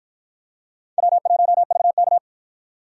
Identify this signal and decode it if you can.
Morse code, heard as a keyed tone, about 53 wpm, 700 Hz. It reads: W1VK